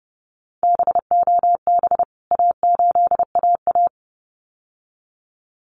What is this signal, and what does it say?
Morse code, heard as a keyed tone, about 30 wpm, 700 Hz. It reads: BOB A8AA
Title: CWT Exchange 30wpm